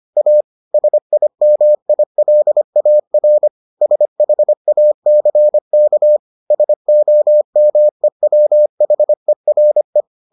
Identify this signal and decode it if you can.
Morse code, heard as a keyed tone, about 25 wpm, 600 Hz. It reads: A SIMILAR SHACK SOMEWHERE